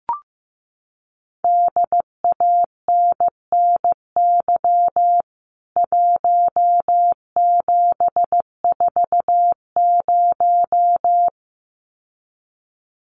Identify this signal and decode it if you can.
Morse code, heard as a keyed tone, about 15 wpm, 700 Hz. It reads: DANNY 1740
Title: CWT Exchange 15wpm